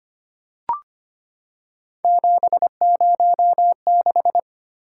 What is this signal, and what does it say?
Morse code, heard as a keyed tone, about 25 wpm, 700 Hz. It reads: 706
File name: Three Letter-Number Combinations - 25wpm